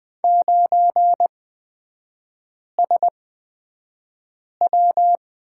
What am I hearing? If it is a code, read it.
Morse code, heard as a keyed tone, about 20 wpm, spaced as for 5 wpm, 700 Hz. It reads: 9SW
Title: 236 Lesson 38e - Letter q - Three Characters 20wpm